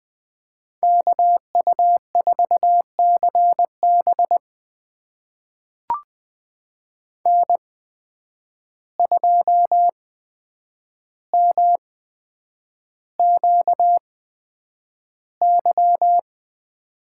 Morse code, heard as a keyed tone, about 20 wpm, spaced as for 9 wpm, 700 Hz. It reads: KU4CB N2MQY
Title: USA Call Signs - 5 Characters - 8x Character Spacing - 20wpm